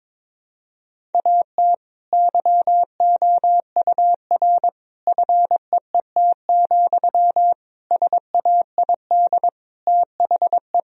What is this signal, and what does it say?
Morse code, heard as a keyed tone, about 21 wpm, 700 Hz. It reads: AT YOUR FEET, SAID THE
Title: Sentences from Top 1000 Words - Review Part 1 - 22wpm